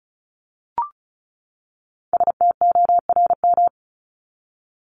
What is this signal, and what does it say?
Morse code, heard as a keyed tone, about 34 wpm, 700 Hz. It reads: STORM
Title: Top 1251-1500 Words 35wpm